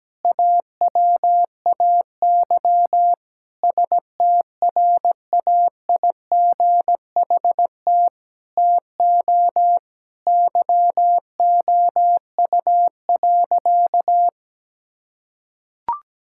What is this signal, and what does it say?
Morse code, heard as a keyed tone, about 17 wpm, 700 Hz. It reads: AWAY STRAIGHT TO YOU.
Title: Sentences from Top 1000 Words - Review Part 6 - 17wpm